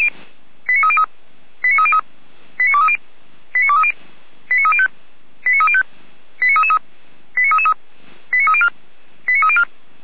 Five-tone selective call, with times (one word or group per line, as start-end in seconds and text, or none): 0.7-1.0 89303
1.6-2.0 89303
2.6-3.0 89230
3.5-3.9 89230
4.5-4.9 89305
5.5-5.8 89305
6.4-6.8 89333
7.4-7.7 89333
8.3-8.7 89304
9.3-9.6 89304